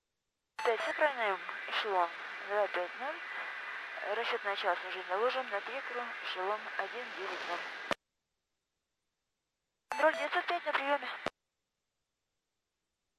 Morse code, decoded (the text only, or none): none